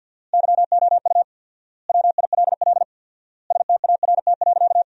Morse code, heard as a keyed tone, about 50 wpm, 700 Hz. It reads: YOU WILL START.
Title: 4x Word-Spacing - Sentences from Top 300 Words - 50wpm